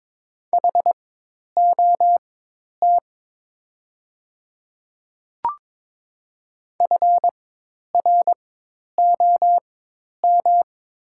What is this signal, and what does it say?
Morse code, heard as a keyed tone, about 22 wpm, spaced as for 10 wpm, 700 Hz. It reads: HOT FROM
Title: 4x Character Spacing - Top 100 Words - 22wpm